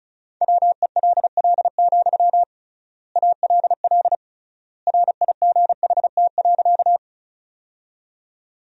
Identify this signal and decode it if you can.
Morse code, heard as a keyed tone, about 34 wpm, 700 Hz. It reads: WELL, ALL RIGHT.
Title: 3x Word-Spacing - Sentences from Top 200 Words - 35wpm